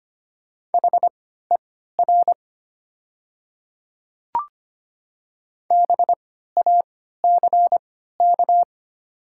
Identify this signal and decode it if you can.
Morse code, heard as a keyed tone, about 25 wpm, spaced as for 14 wpm, 700 Hz. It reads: HER BACK
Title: 3x Character Spacing - Top 200 Words - 25wpm